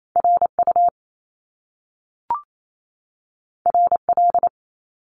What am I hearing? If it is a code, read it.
Morse code, heard as a keyed tone, about 28 wpm, 700 Hz. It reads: RU RL
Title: Top Two Letter Combinations in Words 28wpm